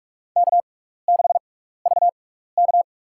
Morse code, heard as a keyed tone, about 43 wpm, spaced as for 15 wpm, 700 Hz. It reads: K6VK
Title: USA Call Signs - 4 Characters - 6x Character Spacing - 45wpm